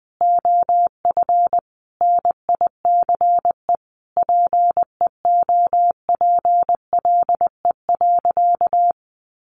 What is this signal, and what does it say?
Morse code, heard as a keyed tone, about 20 wpm, 700 Hz. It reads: OF NICE PEOPLE.
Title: Sentences from Top 1100 Words 20wpm